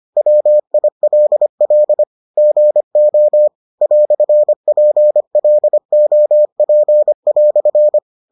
Morse code, heard as a keyed tone, about 25 wpm, 600 Hz. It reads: WILL GO "PLOP"